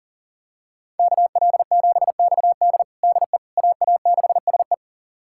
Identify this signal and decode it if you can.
Morse code, heard as a keyed tone, about 37 wpm, 700 Hz. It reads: KL7XD DE AA6SE
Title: USA Callsign DE Callsign - 5 Characters - 40wpm